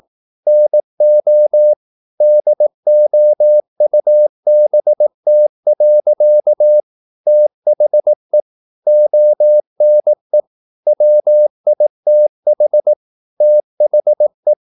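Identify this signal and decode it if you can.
Morse code, heard as a keyed tone, about 18 wpm, 600 Hz. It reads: NO DOUBT. THE ONE WITH THE
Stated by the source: Morse practice at 18 wpm, English text, from CW med Gnister.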